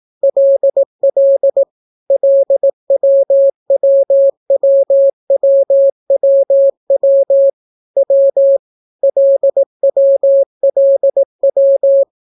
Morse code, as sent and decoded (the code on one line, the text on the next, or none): .-.. .-.. / .-.. .-- .-- .-- .-- .-- .-- / .-- / .-.. .-- .-.. .--
LL LWWWWWW W LWLW